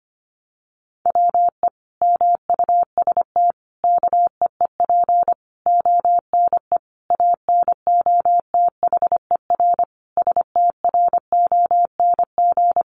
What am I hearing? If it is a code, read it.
Morse code, heard as a keyed tone, about 25 wpm, 700 Hz. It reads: WE MUST KEEP ONE ANOTHER STRONG
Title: Sentences from Top 600 Words 25wpm